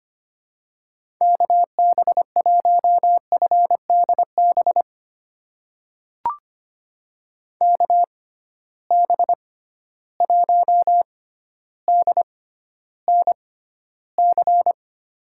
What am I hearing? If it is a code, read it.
Morse code, heard as a keyed tone, about 25 wpm, spaced as for 13 wpm, 700 Hz. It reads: KB1FDB KB1DNC
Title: USA Call Signs - 6 Characters - 6x Character Spacing - 25wpm